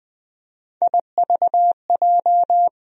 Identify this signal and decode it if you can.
Morse code, heard as a keyed tone, about 20 wpm, 700 Hz. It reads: IVJ
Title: USA Call Sign Postfixes - 3 Letters 20wpm